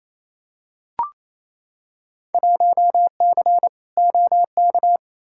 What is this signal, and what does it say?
Morse code, heard as a keyed tone, about 28 wpm, 700 Hz. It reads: 1C OK
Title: Field Day Exchanges Encore - 28wpm